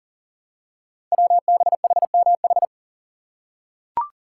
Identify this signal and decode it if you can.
Morse code, heard as a keyed tone, about 42 wpm, 700 Hz. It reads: WBHMH
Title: Code Groups - 5 Letter:Numbers - 40wpm